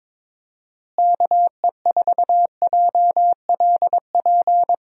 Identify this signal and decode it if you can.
Morse code, heard as a keyed tone, about 22 wpm, 700 Hz. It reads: KE4JLP
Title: USA Call Signs - 6 Characters - Repeated 3x - 22wpm